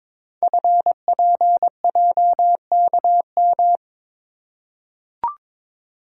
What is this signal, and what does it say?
Morse code, heard as a keyed tone, about 23 wpm, 700 Hz. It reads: FPJKM
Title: Code Groups - 5 Letter:Numbers - 22wpm